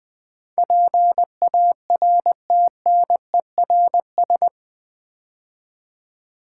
Morse code, heard as a keyed tone, about 21 wpm, 700 Hz. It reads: PARTNERS
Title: Build A Word - 4 Letters At A Time - 11-Letter Words - 20wpm